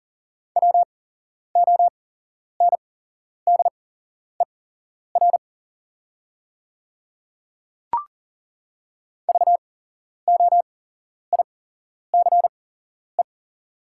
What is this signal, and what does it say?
Morse code, heard as a keyed tone, about 40 wpm, spaced as for 11 wpm, 690 Hz. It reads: WONDER VOICE
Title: 8x Character Spacing - Top 500 Words - 40wpm